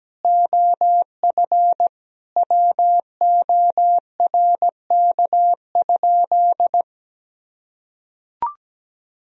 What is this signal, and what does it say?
Morse code, heard as a keyed tone, about 17 wpm, 700 Hz. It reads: OF WORK?
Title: Sentences from Top 1000 Words - Review Part 2 - 17wpm